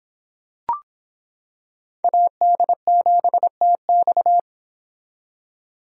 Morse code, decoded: AD7TX

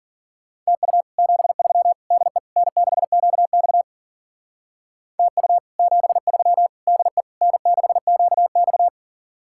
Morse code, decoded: TU 73 DE N6QX TU 73 DE N6QX

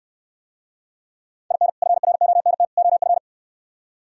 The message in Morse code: IN FRONT OF

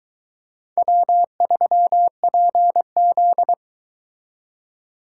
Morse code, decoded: W3PZ

